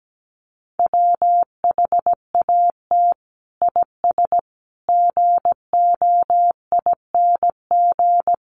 Morse code, heard as a keyed tone, about 17 wpm, 700 Hz. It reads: WHAT IS GOING
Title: Sets of 5 Words Encore2 17wpm